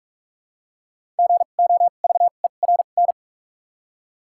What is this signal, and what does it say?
Morse code, heard as a keyed tone, about 46 wpm, spaced as for 32 wpm, 700 Hz. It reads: GOVERN
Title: 2x Character Spacing - Top 500 Words - 45wpm